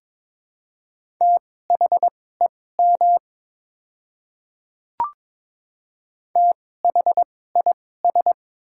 Morse code, heard as a keyed tone, about 22 wpm, spaced as for 16 wpm, 700 Hz. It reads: THEM THIS
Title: 2x Character Spacing - Top 100 Words - 22wpm